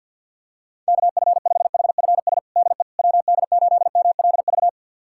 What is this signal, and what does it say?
Morse code, heard as a keyed tone, about 52 wpm, 700 Hz. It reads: KF5HFS DE WD8MLV
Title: USA Callsign DE Callsign - 6 Characters - 50wpm